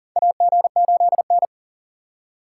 Morse code, heard as a keyed tone, about 40 wpm, 700 Hz. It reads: AG8N
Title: USA Call Signs - 4 Characters - 4x Character Spacing - 40wpm